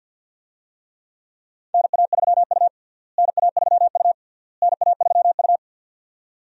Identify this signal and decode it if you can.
Morse code, heard as a keyed tone, about 51 wpm, 700 Hz. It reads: NA3U NA3U NA3U
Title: USA Call Signs - 4 Characters - Repeated 3x - 50wpm